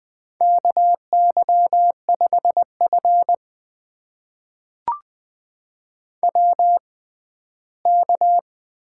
Morse code, heard as a keyed tone, about 20 wpm, spaced as for 14 wpm, 700 Hz. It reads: KY5F WK
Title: USA Call Signs - 4 Characters - 6x Character Spacing - 20wpm